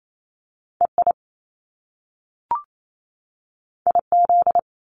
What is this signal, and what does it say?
Morse code, heard as a keyed tone, about 28 wpm, 700 Hz. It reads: EI IZ
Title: Top Two Letter Combinations in Words 28wpm